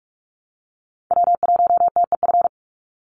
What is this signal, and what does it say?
Morse code, heard as a keyed tone, about 44 wpm, 700 Hz. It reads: W1TEF
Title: Call Signs Repeated 2x - 45wpm